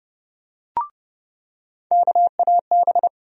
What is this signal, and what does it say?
Morse code, heard as a keyed tone, about 29 wpm, 700 Hz. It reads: KAB
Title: USA Call Sign Postfixes - 3 Letters 30wpm